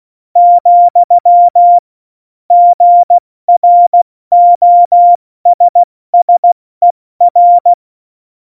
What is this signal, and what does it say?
Morse code, heard as a keyed tone, about 16 wpm, 700 Hz. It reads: , GROSSER
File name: Morsekurs / Witze / Witz 04. September 2018, 70 F80